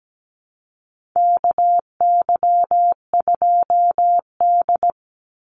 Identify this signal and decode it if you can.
Morse code, heard as a keyed tone, about 17 wpm, 700 Hz. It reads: KY2D
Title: Call Signs - 17wpm